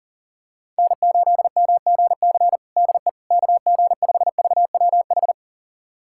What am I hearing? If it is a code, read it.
Morse code, heard as a keyed tone, about 40 wpm, 700 Hz. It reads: N8MGC DE KG5VWH
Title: USA Callsign DE Callsign 40wpm